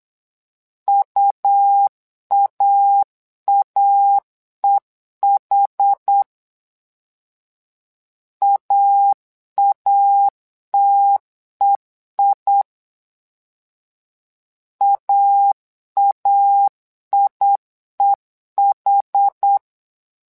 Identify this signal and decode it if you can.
Morse code, heard as a keyed tone, about 8 wpm, 800 Hz. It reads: UAAEH AATEI AAIEH
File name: LEZIONI / 2 UHDN / 2 UH 8wpm 2129